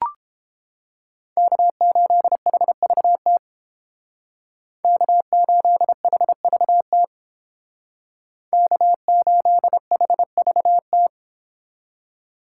Morse code, heard as a keyed tone, about 29 wpm, 700 Hz. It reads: K8HVT K8HVT K8HVT